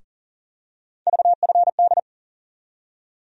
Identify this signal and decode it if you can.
Morse code, heard as a keyed tone, about 40 wpm, 700 Hz. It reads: VFD
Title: USA Call Sign Postfixes - 3 Letters 40wpm